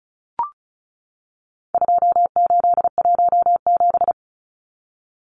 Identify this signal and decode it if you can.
Morse code, heard as a keyed tone, about 34 wpm, 700 Hz. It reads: 2817
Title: Numbers - 4-Digit 35wpm